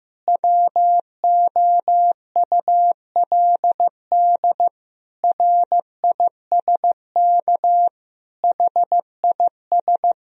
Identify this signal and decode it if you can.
Morse code, heard as a keyed tone, about 15 wpm, 700 Hz. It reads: WOULD RISK HIS
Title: Sentences from Top 1000 Words - Review Part 4 - 15wpm